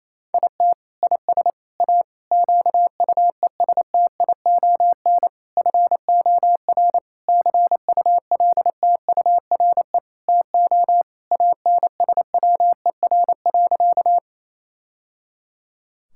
Morse code, heard as a keyed tone, about 28 wpm, 700 Hz. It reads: IT IS A QUESTION FOR CULTURE TO ANSWER.
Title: Sentences from Top 1000 Words - Review Part 6 - 28wpm